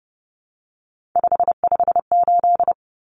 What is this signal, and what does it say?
Morse code, heard as a keyed tone, about 30 wpm, 700 Hz. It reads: 558
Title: Common Cut Numbers - 3-Digit 30wpm